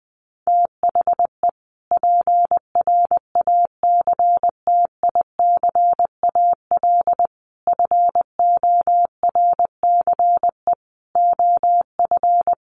Morse code, heard as a keyed tone, about 20 wpm, 700 Hz. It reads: THE PRACTICAL FORCE OF
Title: Sentences from Top 1400 Words 20wpm